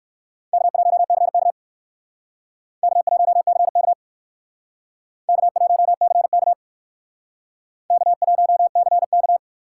Morse code, heard as a keyed tone, about 57 wpm, 700 Hz. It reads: K1CK K1CK K1CK K1CK